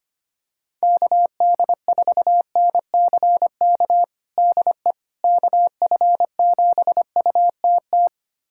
Morse code, heard as a keyed tone, about 25 wpm, 700 Hz. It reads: KD4NCK DE KF7UTT
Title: USA Callsign DE Callsign 25wpm